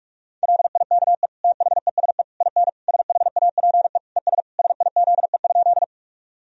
Morse code, heard as a keyed tone, about 46 wpm, 700 Hz. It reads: LIKE THESE IN SHAPE ES SIZE?
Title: Sentences from Top 500 Words and Abbreviations 45wpm